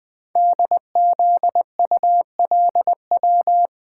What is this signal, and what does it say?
Morse code, heard as a keyed tone, about 20 wpm, 700 Hz. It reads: DZULW
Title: Code Groups - 5 Letters - 20wpm